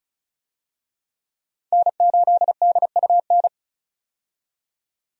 Morse code, heard as a keyed tone, about 34 wpm, 700 Hz. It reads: N8DUN
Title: USA Call Signs - 5 Characters - 35wpm